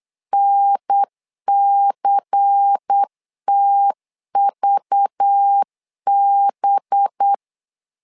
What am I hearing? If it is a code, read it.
Morse code, heard as a keyed tone, about 8 wpm, 800 Hz. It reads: NCTVB
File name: LEZIONI / 5 LBVC / 5 LBVC 8wpm 2935